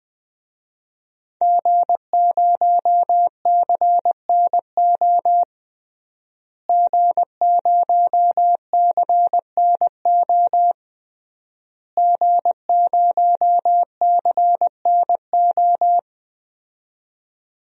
Morse code, heard as a keyed tone, about 20 wpm, 700 Hz. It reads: G0CNO G0CNO G0CNO
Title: DXCC-100 Call Signs - 5 Characters - Repeated 3x - 20wpm